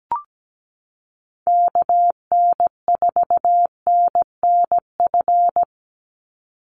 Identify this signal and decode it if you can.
Morse code, heard as a keyed tone, about 17 wpm, 700 Hz. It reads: KN4NNF
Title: USA Call Signs 17wpm